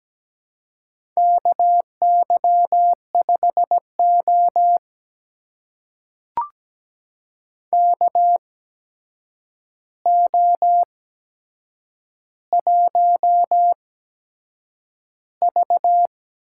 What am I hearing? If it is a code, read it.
Morse code, heard as a keyed tone, about 17 wpm, spaced as for 7 wpm, 700 Hz. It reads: KY5O KO1V
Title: USA Call Signs - 4 Characters - 8x Character Spacing - 17wpm